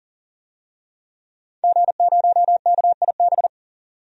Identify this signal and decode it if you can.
Morse code, heard as a keyed tone, about 40 wpm, 700 Hz. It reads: G0KIB
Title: DXCC-100 Call Signs - 5 Characters - 40wpm